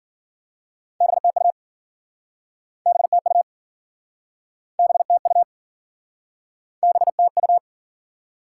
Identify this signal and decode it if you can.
Morse code, heard as a keyed tone, about 53 wpm, 700 Hz. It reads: BTU BTU BTU BTU